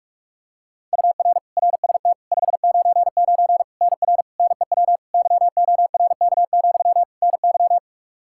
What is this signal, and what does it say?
Morse code, heard as a keyed tone, about 41 wpm, 700 Hz. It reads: UR RST 599 NR NEW YORK, NY
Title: K1USN SST QSO Practice 45wpm